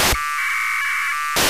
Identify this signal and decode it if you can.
Packet radio (AX.25, as in APRS).
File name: Horyu-4 10h28 UTC AFSK and CW JE9PEL